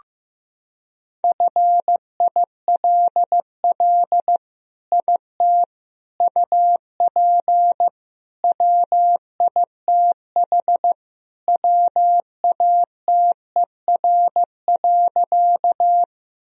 Morse code, heard as a keyed tone, about 16 wpm, 700 Hz. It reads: FILL IT UP WITH WATER.
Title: Sentences from Top 1000 Words - Review Part 3 - 15wpm